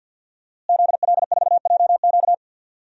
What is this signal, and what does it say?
Morse code, heard as a keyed tone, about 52 wpm, 700 Hz. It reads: ZL4JQ
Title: DXCC-100 Call Signs - 5 Characters - Repeated 3x - 50wpm